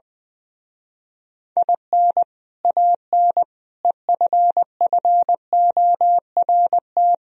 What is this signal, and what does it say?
Morse code, heard as a keyed tone, about 21 wpm, 700 Hz. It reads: IN AN EFFORT
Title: Speed-Racing - Sets of 4 Words 20wpm